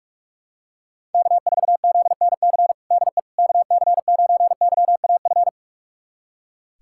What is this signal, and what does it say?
Morse code, heard as a keyed tone, about 46 wpm, 690 Hz. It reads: K4ZNC DE KC9YAF
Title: USA Callsign DE Callsign 45wpm